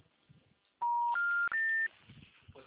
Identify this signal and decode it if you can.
Special information tone.
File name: out- -2022- - - .25637